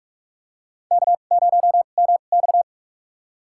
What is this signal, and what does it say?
Morse code, heard as a keyed tone, about 42 wpm, spaced as for 31 wpm, 700 Hz. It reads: K0MX
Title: USA Call Signs - 4 Characters - 2x Character Spacing - 45wpm